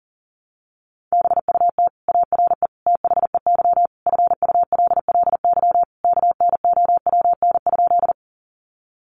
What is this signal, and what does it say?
Morse code, heard as a keyed tone, about 37 wpm, 700 Hz. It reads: BUT ARE THEY FULLY KNOWN?
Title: Sentences from Top 1300 Words 40wpm